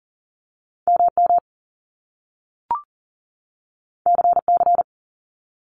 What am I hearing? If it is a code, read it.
Morse code, heard as a keyed tone, about 38 wpm, 700 Hz. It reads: MM CC
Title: Top Two Letter Combinations in Words 40wpm